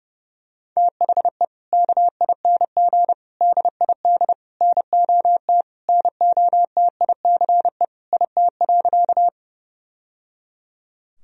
Morse code, heard as a keyed tone, about 30 wpm, 700 Hz. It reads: THE KING DID NOT NOTICE IT.
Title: Sentences from Top 1000 Words - Review Part 2 - 30wpm